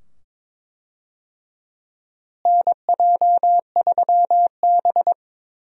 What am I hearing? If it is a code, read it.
Morse code, heard as a keyed tone, about 22 wpm, 700 Hz. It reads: NJ3B
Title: USA Call Signs - 4 Characters - Repeated 2x - 22wpm